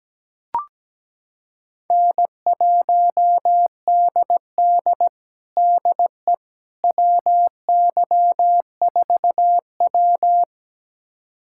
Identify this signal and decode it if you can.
Morse code, heard as a keyed tone, about 17 wpm, 700 Hz. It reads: N1DD DE WY4W
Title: USA Callsign DE Callsign - 4 Characters - 17wpm